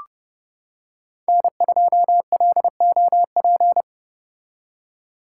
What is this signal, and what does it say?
Morse code, heard as a keyed tone, about 30 wpm, 700 Hz. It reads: N2LOP